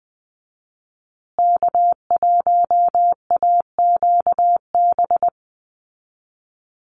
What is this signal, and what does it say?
Morse code, heard as a keyed tone, about 20 wpm, 700 Hz. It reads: K1AQB
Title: Call Signs Repeated 2x - 20wpm